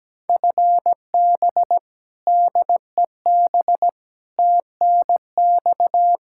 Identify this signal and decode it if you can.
Morse code, heard as a keyed tone, about 17 wpm, 700 Hz. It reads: FB DEB TNX
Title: K1USN SST QSO Practice 17wpm